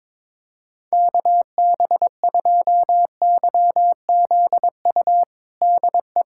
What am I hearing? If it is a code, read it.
Morse code, heard as a keyed tone, about 22 wpm, 700 Hz. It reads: KB2YZU DE
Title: USA Callsign DE Callsign 22wpm